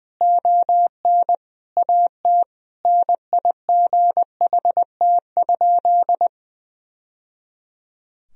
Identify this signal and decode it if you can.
Morse code, heard as a keyed tone, about 20 wpm, 700 Hz. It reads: ON AT NIGHT?
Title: Sentences from Top 1000 Words - Review Part 4 - 20wpm